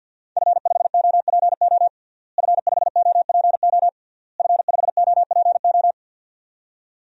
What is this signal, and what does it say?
Morse code, heard as a keyed tone, about 52 wpm, 700 Hz. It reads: F5OPO F5OPO F5OPO
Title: DXCC-100 Call Signs - 5 Characters - Repeated 3x - 50wpm